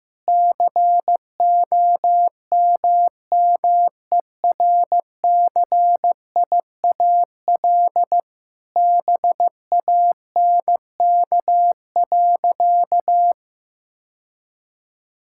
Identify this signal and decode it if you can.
Morse code, heard as a keyed tone, about 15 wpm, 700 Hz. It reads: COMMERCIAL BANK.
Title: Sentences from Top 1000 Words - Review Part 6 - 15wpm